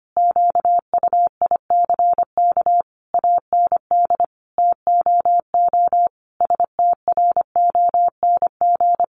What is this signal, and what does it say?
Morse code, heard as a keyed tone, about 25 wpm, 700 Hz. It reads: QUICK AND TOO STRONG
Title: Sentences from Top 600 Words 25wpm